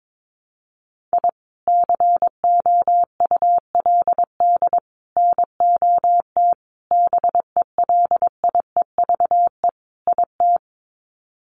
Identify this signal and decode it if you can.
Morse code, heard as a keyed tone, about 22 wpm, 700 Hz. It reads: I COULD NOT BELIEVE IT
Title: Sets of 5 Words Encore 22wpm